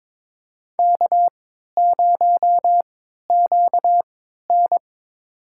Morse code, heard as a keyed tone, about 21 wpm, spaced as for 12 wpm, 700 Hz. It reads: K0QN
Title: USA Call Signs - 4 Characters - 3x Character Spacing - 22wpm